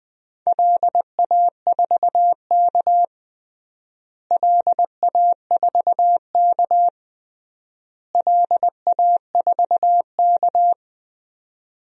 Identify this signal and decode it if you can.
Morse code, heard as a keyed tone, about 20 wpm, 700 Hz. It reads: LA4K LA4K LA4K
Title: DXCC-100 Call Signs - 4 Characters - Repeated 3x - 20wpm